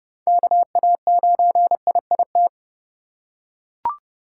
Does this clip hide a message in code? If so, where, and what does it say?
Yes: Morse code from 0.3 to 2.5 s, heard as a keyed tone, about 26 wpm, 700 Hz. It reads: KA9IIT